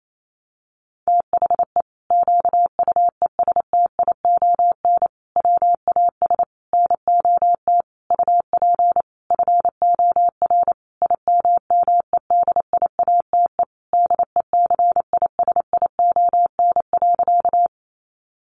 Morse code, heard as a keyed tone, about 28 wpm, 700 Hz. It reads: THE QUESTION WAS NOT UP FOR IMMEDIATE DECISION.
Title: Sentences from Top 1200 Words 28wpm